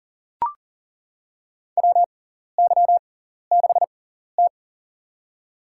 Morse code, heard as a keyed tone, about 40 wpm, spaced as for 14 wpm, 700 Hz. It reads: WY6T
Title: USA Call Signs - 4 Characters - 6x Character Spacing - 40wpm